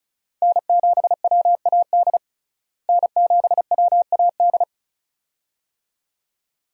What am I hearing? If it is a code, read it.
Morse code, heard as a keyed tone, about 36 wpm, 700 Hz. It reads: N7WAD N7WAD
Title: USA Call Signs - 5 Characters - Repeated 2x - 35wpm